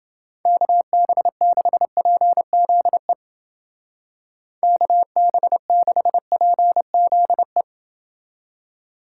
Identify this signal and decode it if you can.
Morse code, heard as a keyed tone, about 30 wpm, 700 Hz. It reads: KB6PZE KB6PZE